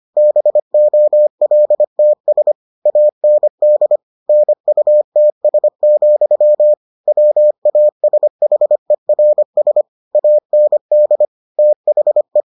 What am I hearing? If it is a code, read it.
Morse code, heard as a keyed tone, about 25 wpm, 600 Hz. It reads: BOLTS AND NUTS, WASHERS AND THE